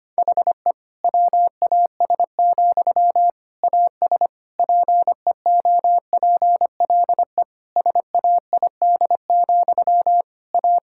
Morse code, heard as a keyed tone, about 25 wpm, 700 Hz. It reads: HE WAS, AS PEOPLE SAID, A
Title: Sentences from Top 1000 Words - Review Part 6 - 25wpm